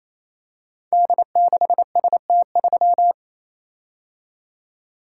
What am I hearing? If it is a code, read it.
Morse code, heard as a keyed tone, about 27 wpm, 700 Hz. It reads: D6ST3
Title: Code Groups - 5 Letter:Numbers - 28wpm